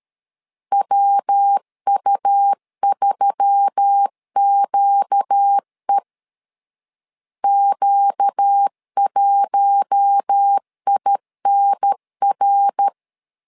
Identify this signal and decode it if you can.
Morse code, heard as a keyed tone, about 13 wpm, 800 Hz. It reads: WU3QE Q1INR
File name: LEZIONI / 4 Q1S3 12wpm 2023